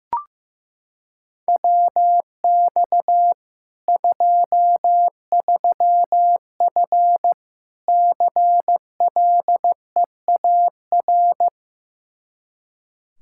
Morse code, heard as a keyed tone, about 15 wpm, 700 Hz. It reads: WX 23F CLEAR
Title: K1USN SST QSO Practice 15wpm